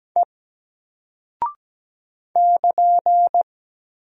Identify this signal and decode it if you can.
Morse code, heard as a keyed tone, about 17 wpm, 700 Hz. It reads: E <KN>
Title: Single Character (Letter:Number:Punctuation:Prosign) Alt 17wpm